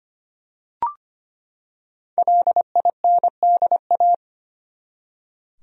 Morse code, heard as a keyed tone, about 25 wpm, 700 Hz. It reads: LINDA
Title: USA Common Names - 25wpm